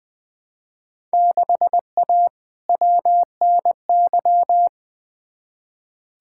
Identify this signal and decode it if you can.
Morse code, heard as a keyed tone, about 20 wpm, 700 Hz. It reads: 6A WNY
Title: Field Day Exchanges Encore - 20wpm